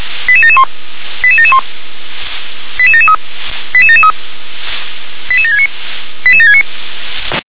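Five-tone selective call (ZVEI), heard as quick runs of five tones, tones of about 70 ms. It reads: 88771 88771 88773 88773 88769 88769
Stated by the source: ZVEI five-tone selective call received with an SDR.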